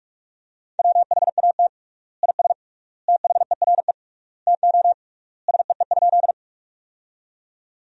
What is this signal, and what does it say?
Morse code, heard as a keyed tone, about 45 wpm, 700 Hz. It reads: WHAT IS THERE TO SEE?
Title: 3x Word-Spacing - Sentences from Top 100 Words - 45wpm